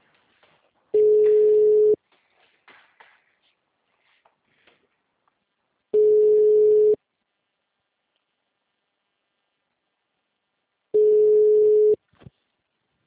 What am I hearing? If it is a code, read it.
Ringback tone (Europe, 425 Hz).